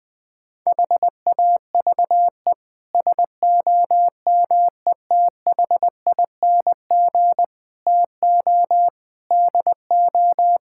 Morse code, heard as a keyed tone, about 20 wpm, 700 Hz. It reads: HAVE SOMETHING TO DO
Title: Sets of 8 Words - 20wpm